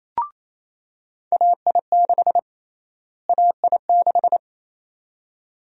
Morse code, heard as a keyed tone, about 28 wpm, 700 Hz. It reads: AI6 AI6
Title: USA Call Sign Prefixes - Repeated 2x - 28wpm